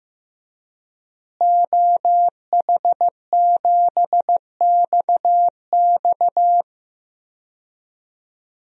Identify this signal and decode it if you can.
Morse code, heard as a keyed tone, about 15 wpm, 700 Hz. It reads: OH7XX
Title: DXCC-100 Call Signs - 5 Characters - Repeated 2x - 15wpm